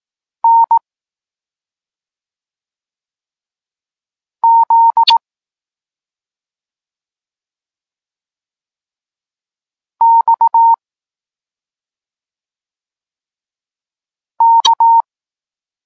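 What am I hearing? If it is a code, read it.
Morse code, heard as a keyed tone, about 18 wpm, 920 Hz. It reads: N Z X K
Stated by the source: Morse code lesson for learning the letters K and X, from the SP9MOA Niepołomicki Klub Krótkofalowców course.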